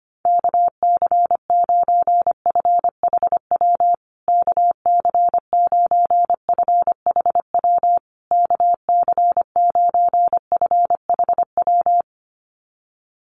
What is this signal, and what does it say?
Morse code, heard as a keyed tone, about 25 wpm, 700 Hz. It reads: KC9FHW KC9FHW KC9FHW
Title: USA Call Signs Repeated 3x 25wpm